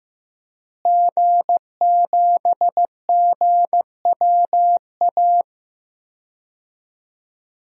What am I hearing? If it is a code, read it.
Morse code, heard as a keyed tone, about 15 wpm, 700 Hz. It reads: G7GWA
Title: DXCC-100 Call Signs - 5 Characters - Repeated 3x - 15wpm